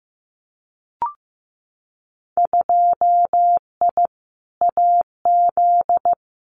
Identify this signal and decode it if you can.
Morse code, heard as a keyed tone, about 15 wpm, 700 Hz. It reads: 2I AZ
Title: Winter Field Day Exchanges Encore 15wpm